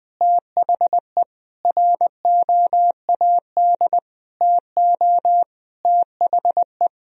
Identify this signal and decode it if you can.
Morse code, heard as a keyed tone, about 20 wpm, 700 Hz. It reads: THE ROAD TO THE
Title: Sentences from Top 1000 Words - Review Part 2 - 20wpm